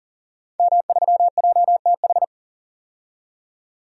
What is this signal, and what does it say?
Morse code, heard as a keyed tone, about 40 wpm, 700 Hz. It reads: M3JTH
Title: DXCC-100 Call Signs - 5 Characters - 40wpm